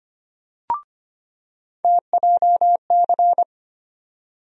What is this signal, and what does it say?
Morse code, heard as a keyed tone, about 25 wpm, 700 Hz. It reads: TJC
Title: USA Call Sign Postfixes - 3 Letters 25wpm